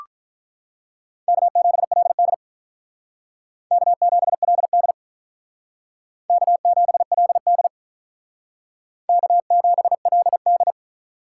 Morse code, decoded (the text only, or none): K7LD K7LD K7LD K7LD